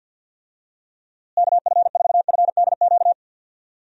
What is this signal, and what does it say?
Morse code, heard as a keyed tone, about 46 wpm, 700 Hz. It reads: KF4FDQ
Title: USA Call Signs - 6 Characters - Repeated 3x - 50wpm